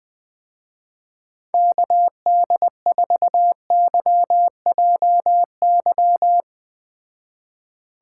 Morse code, heard as a keyed tone, about 20 wpm, 700 Hz. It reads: KD4YJY
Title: Speed Racing - USA Call Signs 17wpm